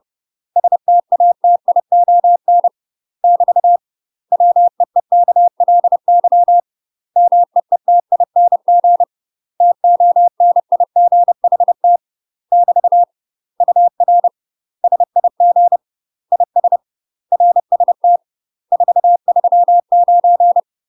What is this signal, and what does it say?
Morse code, heard as a keyed tone, about 30 wpm, 700 Hz. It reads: STATION <BT> WEEKLY MEETING TONIGHT <BT> UR SIG IS RST 439
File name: UK MASTER QSOs TRAINING / 30WPM / UK MASTER QSOs 0035